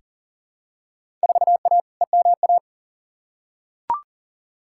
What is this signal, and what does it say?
Morse code, heard as a keyed tone, about 40 wpm, 700 Hz. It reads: 4A EMA
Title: Field Day Exchanges - 40wpm